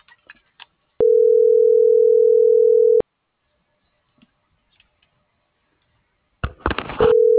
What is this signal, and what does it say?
Ringback tone (North America), 440 + 480 Hz.